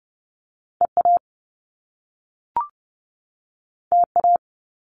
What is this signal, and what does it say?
Morse code, heard as a keyed tone, about 30 wpm, 700 Hz. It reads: EA TA
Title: Top Two Letter Combinations in Words 30wpm